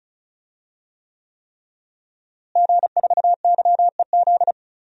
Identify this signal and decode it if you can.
Morse code, heard as a keyed tone, about 34 wpm, 700 Hz. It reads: G4YEZ
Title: DXCC-100 Call Signs - 5 Characters - Repeated 3x - 35wpm